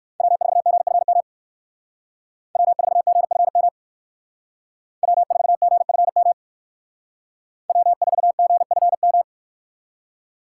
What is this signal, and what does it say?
Morse code, heard as a keyed tone, about 60 wpm, 700 Hz. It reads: W4GFM W4GFM W4GFM W4GFM